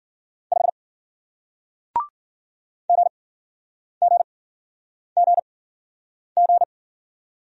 Morse code, heard as a keyed tone, about 47 wpm, spaced as for 9 wpm, 700 Hz. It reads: 5 GGGG
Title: Speed-Racing Single Letter-Number 40wpm